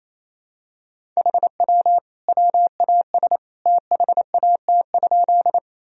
Morse code, heard as a keyed tone, about 28 wpm, 700 Hz. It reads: HW WAS THAT?
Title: Sentences from Top 100 Words and Abbreviations 28wpm